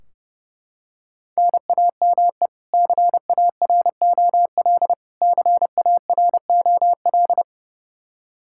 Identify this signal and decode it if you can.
Morse code, heard as a keyed tone, about 30 wpm, 700 Hz. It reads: NAME CAROL CAROL
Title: Practice QSO - Exchange 1 - Beginning - 30wpm